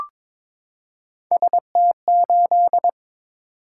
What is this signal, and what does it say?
Morse code, heard as a keyed tone, about 22 wpm, 700 Hz. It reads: ST8
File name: practice / 22wpm / Three Letter-Number Combinations - 22wpm